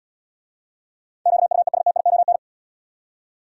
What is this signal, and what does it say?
Morse code, heard as a keyed tone, about 80 wpm, 700 Hz. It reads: QUESTION